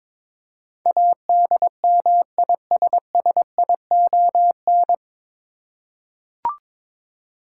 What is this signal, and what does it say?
Morse code, heard as a keyed tone, about 22 wpm, 700 Hz. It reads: ADMISSION
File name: 9 Letter Words - First 3 Letters Given - 22wpm